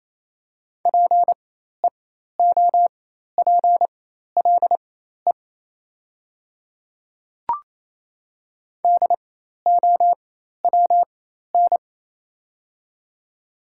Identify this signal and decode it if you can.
Morse code, heard as a keyed tone, about 28 wpm, spaced as for 13 wpm, 700 Hz. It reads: PEOPLE DOWN
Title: 4x Character Spacing - Top 100 Words - 28wpm